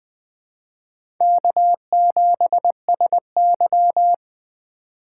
Morse code, heard as a keyed tone, about 20 wpm, 700 Hz. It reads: K7SY